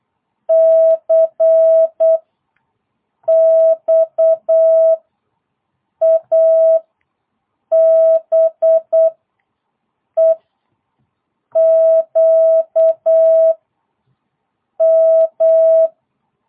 Morse code, heard as a keyed tone, about 8 wpm, 650 Hz. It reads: C X A B E Q M